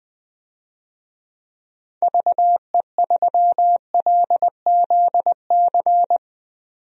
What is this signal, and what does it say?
Morse code, heard as a keyed tone, about 20 wpm, 700 Hz. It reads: VE3LZC
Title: DXCC-100 Call Signs - 6 Characters - 20wpm